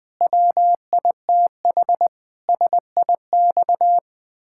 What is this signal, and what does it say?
Morse code, heard as a keyed tone, about 20 wpm, 700 Hz. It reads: WITH SIX
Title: Pangram Challenge v2 - 20wpm